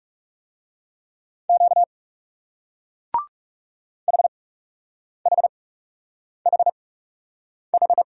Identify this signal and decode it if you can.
Morse code, heard as a keyed tone, about 39 wpm, spaced as for 8 wpm, 700 Hz. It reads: Q HHHH